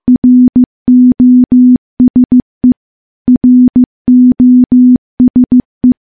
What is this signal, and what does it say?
Morse code, heard as a keyed tone, about 15 wpm, 250 Hz. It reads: ROSE ROSE
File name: morse / morse-training-15WPM